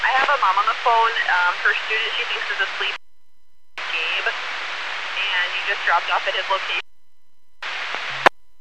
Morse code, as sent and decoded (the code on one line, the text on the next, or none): none